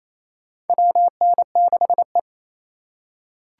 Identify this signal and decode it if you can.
Morse code, heard as a keyed tone, about 27 wpm, 700 Hz. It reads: WN6E
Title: USA Call Signs - 4 Characters - 2x Character Spacing - 28wpm